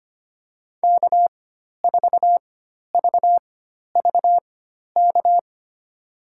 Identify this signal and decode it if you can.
Morse code, heard as a keyed tone, about 25 wpm, spaced as for 12 wpm, 700 Hz. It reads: K4VVK
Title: USA Call Signs - 5 Characters - 4x Character Spacing - 25wpm